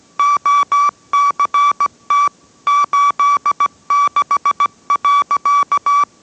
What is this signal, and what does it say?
Morse code, heard as a keyed tone, about 17 wpm, 1150 Hz. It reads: OCT 86.